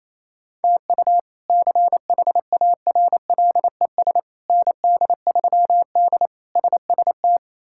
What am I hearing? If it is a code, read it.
Morse code, heard as a keyed tone, about 28 wpm, 700 Hz. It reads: TU CHARLES ND3D SST
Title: K1USN SST QSO Practice - 28wpm